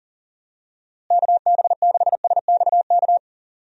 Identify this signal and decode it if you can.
Morse code, heard as a keyed tone, about 40 wpm, 700 Hz. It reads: KB6SXK
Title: USA Call Signs - 6 Characters - Repeated 2x - 40wpm